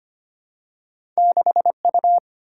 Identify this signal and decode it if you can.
Morse code, heard as a keyed tone, about 25 wpm, 700 Hz. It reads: 6U
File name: Two Letter-Number Combinations - 25wpm